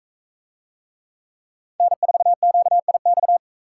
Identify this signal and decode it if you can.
Morse code, heard as a keyed tone, about 40 wpm, 700 Hz. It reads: N4QIX